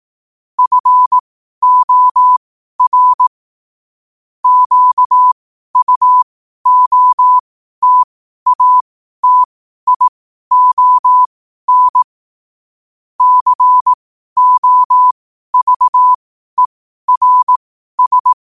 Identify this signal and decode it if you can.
Morse code, heard as a keyed tone, about 18 wpm, spaced as for 12 wpm, 1000 Hz. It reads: FOR QUOTATION COVERS